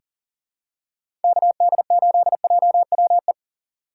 Morse code, heard as a keyed tone, about 37 wpm, 700 Hz. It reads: KD8JWE
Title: USA Call Signs - 6 Characters - 3x Character Spacing - 40wpm